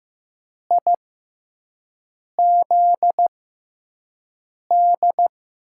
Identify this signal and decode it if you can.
Morse code, heard as a keyed tone, about 15 wpm, spaced as for 5 wpm, 700 Hz. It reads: IZD
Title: USA Call Signs - 6x Character Spacing - 15wpm